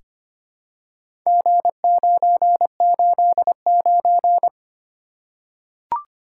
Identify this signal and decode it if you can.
Morse code, heard as a keyed tone, about 26 wpm, 700 Hz. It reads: G989
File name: practice / 25wpm / Four Letter-Number Combinations - 25wpm